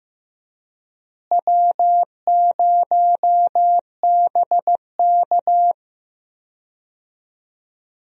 Morse code, heard as a keyed tone, about 15 wpm, 700 Hz. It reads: W0BK
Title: USA Call Signs - 4 Characters - Repeated 2x - 15wpm